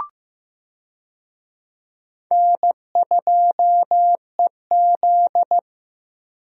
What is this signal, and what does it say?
Morse code, heard as a keyed tone, about 15 wpm, 700 Hz. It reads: N2EZ
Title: USA Call Signs - 4 Characters - Repeated 3x - 15wpm